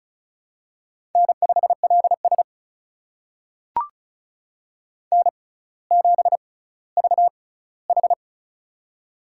Morse code, heard as a keyed tone, about 36 wpm, spaced as for 18 wpm, 700 Hz. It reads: N5LS N7VH
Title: USA Call Signs - 4 Characters - 6x Character Spacing - 35wpm